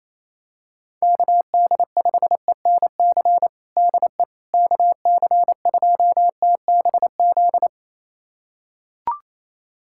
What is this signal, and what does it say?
Morse code, heard as a keyed tone, about 28 wpm, 700 Hz. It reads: KD5ENC DE KC2TBZ
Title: USA Callsign DE Callsign - 6 Characters - 28wpm